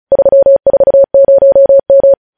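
Morse code, heard as a keyed tone, about 36 wpm, 550 Hz. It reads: 340M
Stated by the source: Morse news headlines practice at 35.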